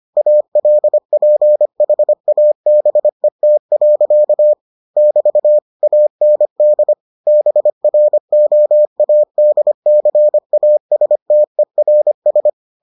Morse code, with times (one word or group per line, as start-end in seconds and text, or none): 0.2-4.5 ALPHABET.
5.0-5.6 <BT>
5.8-6.9 AND
7.3-12.5 BROADCASTERS